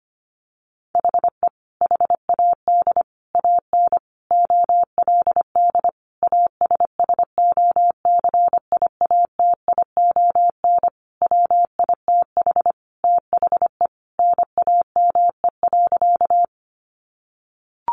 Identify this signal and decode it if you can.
Morse code, heard as a keyed tone, about 25 wpm, 700 Hz. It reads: HE HAD AN OLD ASSOCIATION WITH THE NAME.
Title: Sentences from Top 1100 Words 25wpm